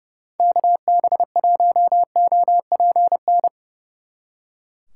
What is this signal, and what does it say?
Morse code, heard as a keyed tone, about 35 wpm, 700 Hz. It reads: KB1OPN
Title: Speed Racing - USA 2x3 Call Signs 30wpm